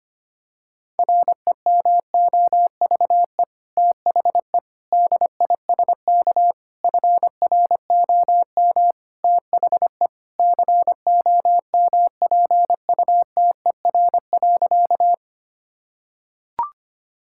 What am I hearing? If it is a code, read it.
Morse code, heard as a keyed tone, about 25 wpm, 700 Hz. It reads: REMOVE THE DISK FROM THE COMPUTER.
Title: Sentences from Top 1000 Words - Review Part 6 - 25wpm